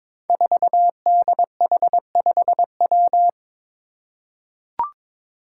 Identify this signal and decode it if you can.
Morse code, heard as a keyed tone, about 22 wpm, 700 Hz. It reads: 4DH5W
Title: Five Challenging Letter-Number Combinations 22wpm